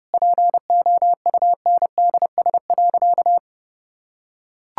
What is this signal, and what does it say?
Morse code, heard as a keyed tone, about 30 wpm, 700 Hz. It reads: POUNDS.
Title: Sentences from Top 1000 Words - Review Part 6 - 30wpm